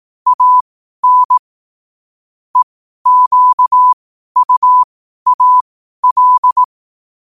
Morse code, heard as a keyed tone, about 18 wpm, spaced as for 12 wpm, 1000 Hz. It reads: AN EQUAL
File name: qotdfiles / 12wpm